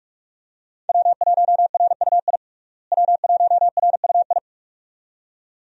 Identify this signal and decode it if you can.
Morse code, heard as a keyed tone, about 46 wpm, 700 Hz. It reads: W1RUI W1RUI
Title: USA Call Signs - 5 Characters - Repeated 2x - 45wpm